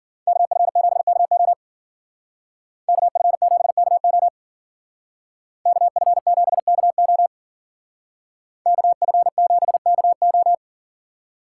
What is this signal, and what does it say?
Morse code, heard as a keyed tone, about 48 wpm, 700 Hz. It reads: KF7KO KF7KO KF7KO KF7KO
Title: Speed Racing - USA 2x2 Call Signs 40wpm